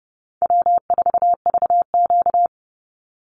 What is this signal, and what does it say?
Morse code, heard as a keyed tone, about 30 wpm, 700 Hz. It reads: W4VQ
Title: Easy Call Signs - 30wpm